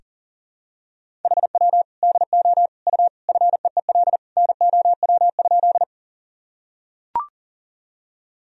Morse code, heard as a keyed tone, about 38 wpm, 700 Hz. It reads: HW DO U FEEL NOW?
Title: Sentences from Top 400 Words and Abbreviations 40wpm